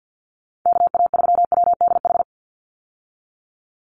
Morse code, heard as a keyed tone, about 48 wpm, 700 Hz. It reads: KA3WDH
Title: USA Call Signs 50wpm